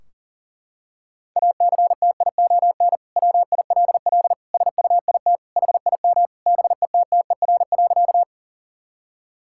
Morse code, heard as a keyed tone, about 40 wpm, 700 Hz. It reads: ACTION WILL SUIT HIM BETTER.
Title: Sentences from Top 1600 Words 40wpm